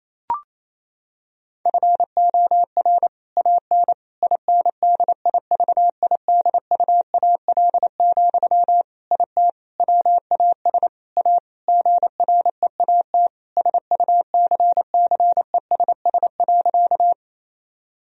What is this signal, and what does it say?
Morse code, heard as a keyed tone, about 28 wpm, 700 Hz. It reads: FOR AN INDIVIDUAL, IT WAS A GREAT SUCCESS.
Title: Sentences from Top 1000 Words - Review Part 4 - 28wpm